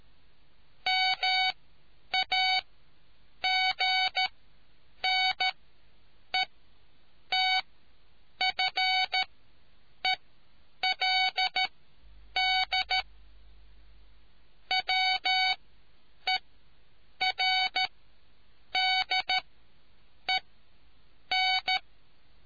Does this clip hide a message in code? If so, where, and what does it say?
Yes: Morse code from 0.9 to 21.8 s, heard as a keyed tone, about 13 wpm, spaced as for 8 wpm, 750 Hz. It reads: MAGNETFELD WERDEN